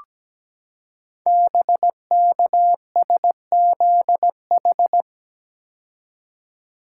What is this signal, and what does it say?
Morse code, heard as a keyed tone, about 17 wpm, 700 Hz. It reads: BKSZH
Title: Code Groups - 5 Letters - 17wpm